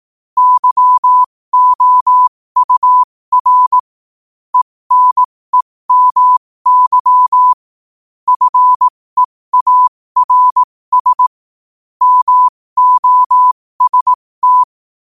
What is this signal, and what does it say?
Morse code, heard as a keyed tone, about 18 wpm, spaced as for 15 wpm, 1000 Hz. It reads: YOUR ENEMY FEARS MOST
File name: qotdfiles / 15wpm